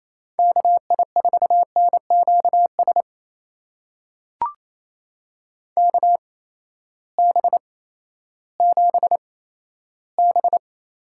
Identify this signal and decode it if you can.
Morse code, heard as a keyed tone, about 28 wpm, spaced as for 14 wpm, 700 Hz. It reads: KI4NQS KB7B
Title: USA Call Signs - 6 Characters - 8x Character Spacing - 28wpm